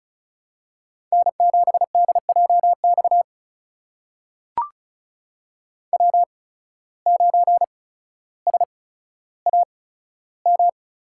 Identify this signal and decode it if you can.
Morse code, heard as a keyed tone, about 34 wpm, spaced as for 15 wpm, 700 Hz. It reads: N7DJX W9SAM
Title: USA Call Signs - 5 Characters - 8x Character Spacing - 35wpm